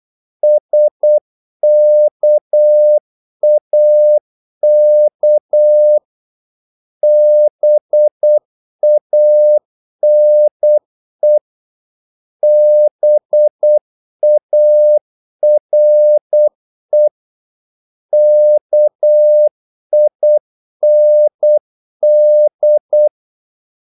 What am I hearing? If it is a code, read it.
Morse code, heard as a keyed tone, about 8 wpm, 600 Hz. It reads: SKAK BANE BARE KIND
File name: Korte ord DK 8wpm